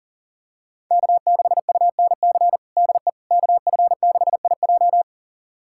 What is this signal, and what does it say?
Morse code, heard as a keyed tone, about 40 wpm, 700 Hz. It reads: K6UNC DE KF6IJ
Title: USA Callsign DE Callsign - 5 Characters - 40wpm